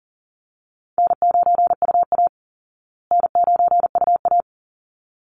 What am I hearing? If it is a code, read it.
Morse code, heard as a keyed tone, about 40 wpm, 700 Hz. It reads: N9UA N9UA
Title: Call Signs Repeated 3x - 40wpm